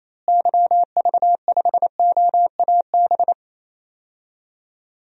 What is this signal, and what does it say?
Morse code, heard as a keyed tone, about 28 wpm, 700 Hz. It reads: YV5OAB
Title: DXCC-100 Call Signs - 6 Characters - Repeated 3x - 28wpm